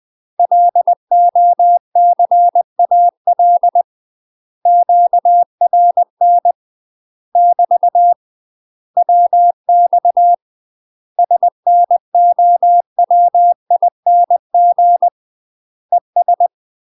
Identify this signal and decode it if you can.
Morse code, heard as a keyed tone, about 20 wpm, 700 Hz. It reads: LOCAL QRN <BT> WX SNOWING ES
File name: UK MASTER QSOs TRAINING / 20WPM / UK MASTER QSOs 0002